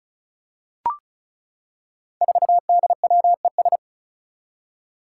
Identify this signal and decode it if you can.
Morse code, heard as a keyed tone, about 36 wpm, 700 Hz. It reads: 4DWES
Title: Five Challenging Letter-Number Combinations 35wpm